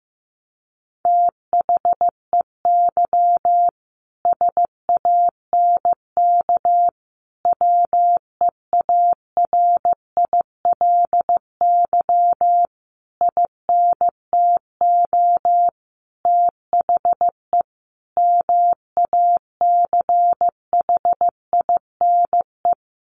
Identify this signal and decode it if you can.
Morse code, heard as a keyed tone, about 15 wpm, 700 Hz. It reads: THEY SANK WEARILY INTO THE MACHINE
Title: A Sound Of Thunder 15wpm